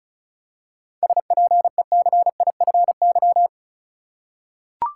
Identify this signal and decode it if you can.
Morse code, heard as a keyed tone, about 34 wpm, 700 Hz. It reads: SPECIFY
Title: Seven Letter Words - 35wpm